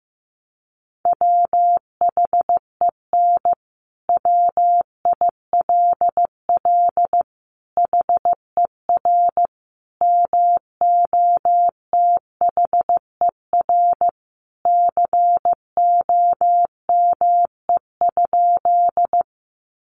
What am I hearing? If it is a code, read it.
Morse code, heard as a keyed tone, about 15 wpm, 700 Hz. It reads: WHEN WILL HER MOTHER COME?
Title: Sentences from Top 200 Words 15wpm